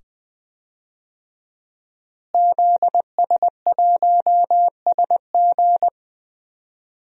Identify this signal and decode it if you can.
Morse code, heard as a keyed tone, about 20 wpm, 700 Hz. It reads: ZS1SG
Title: DXCC-100 Call Signs - 5 Characters - Repeated 2x - 20wpm